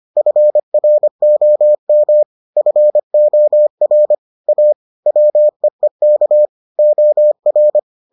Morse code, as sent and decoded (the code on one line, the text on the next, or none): ..-. .-. --- -- / ..-. --- .-. / .- / .-- . -.- / --- .-.
FROM FOR A WEEK OR